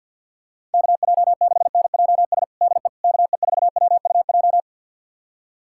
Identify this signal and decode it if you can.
Morse code, heard as a keyed tone, about 53 wpm, 700 Hz. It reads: KJ6NJS DE KE4WUJ
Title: USA Callsign DE Callsign 50wpm